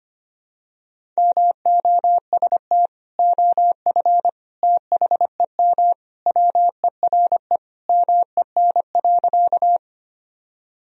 Morse code, heard as a keyed tone, about 25 wpm, 700 Hz. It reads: MOST OF THEM WERE MEN.
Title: Sentences from Top 1000 Words - Review Part 1 - 25wpm